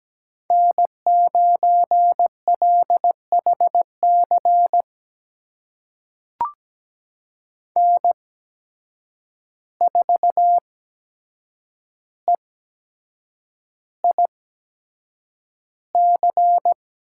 Morse code, heard as a keyed tone, about 17 wpm, spaced as for 7 wpm, 700 Hz. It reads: N9LHC N4EIC
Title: USA Call Signs - 5 Characters - 8x Character Spacing - 17wpm